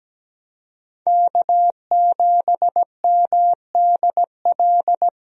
Morse code, heard as a keyed tone, about 17 wpm, 700 Hz. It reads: K7MDL
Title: USA Call Signs - 5 Characters - 17wpm